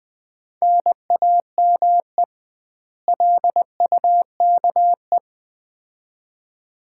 Morse code, heard as a keyed tone, about 20 wpm, 700 Hz. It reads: NAME LUKE
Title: Common QSO Phrases - 2x Word-Spacing - 20wpm